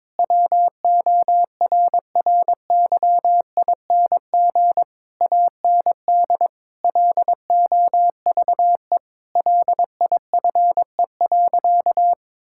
Morse code, heard as a keyed tone, about 22 wpm, 700 Hz. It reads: WORRYING AND LOVE LIFE.
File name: practice / 22wpm / Sentences with 8-Word Phrases - 22wpm